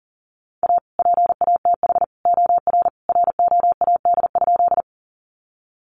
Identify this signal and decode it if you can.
Morse code, heard as a keyed tone, about 40 wpm, 700 Hz. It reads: A PATH OR ROAD?
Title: Sentences from Top 1400 Words 40wpm